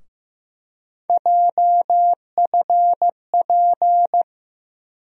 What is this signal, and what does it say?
Morse code, heard as a keyed tone, about 15 wpm, 700 Hz. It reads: JFP
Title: USA Call Sign Postfixes - 3 Letters 15wpm